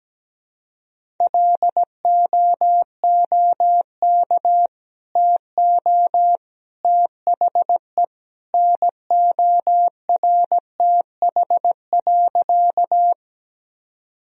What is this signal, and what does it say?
Morse code, heard as a keyed tone, about 17 wpm, 700 Hz. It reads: LOOK TO THE NORTH.
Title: Sentences from Top 1000 Words - Review Part 1 - 17wpm